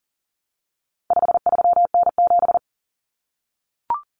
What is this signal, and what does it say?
Morse code, heard as a keyed tone, about 40 wpm, 700 Hz. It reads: 53N7
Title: Common Cut Numbers - 4-Digit 40wpm